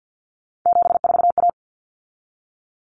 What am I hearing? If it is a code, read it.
Morse code, heard as a keyed tone, about 50 wpm, 700 Hz. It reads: 74A
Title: Common Cut Numbers - 3-Digit 50wpm